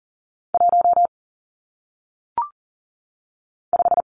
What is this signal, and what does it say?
Morse code, heard as a keyed tone, about 38 wpm, 700 Hz. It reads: 1 5
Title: Numbers - 1-Digit 40wpm